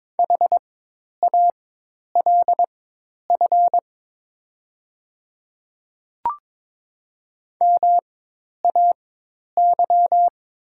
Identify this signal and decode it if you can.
Morse code, heard as a keyed tone, about 22 wpm, spaced as for 10 wpm, 700 Hz. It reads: HALF MAY
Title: 4x Character Spacing - Top 500 Words - 22wpm